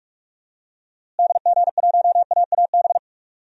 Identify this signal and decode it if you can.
Morse code, heard as a keyed tone, about 42 wpm, 700 Hz. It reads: DG1AAB
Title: DXCC-100 Call Signs - 6 Characters - Repeated 2x - 45wpm